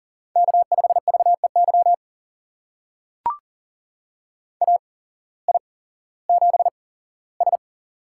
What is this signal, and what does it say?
Morse code, heard as a keyed tone, about 40 wpm, spaced as for 19 wpm, 700 Hz. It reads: K5VEY AI7S
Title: USA Call Signs - 8x Character Spacing - 40wpm